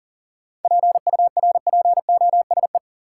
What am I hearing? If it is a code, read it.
Morse code, heard as a keyed tone, about 40 wpm, 700 Hz. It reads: PURPOSE
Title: Sentences from Top 800 Words - New Word Drill - 40wpm